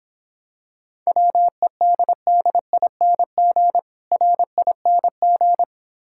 Morse code, heard as a keyed tone, about 26 wpm, 700 Hz. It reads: WEDDING RING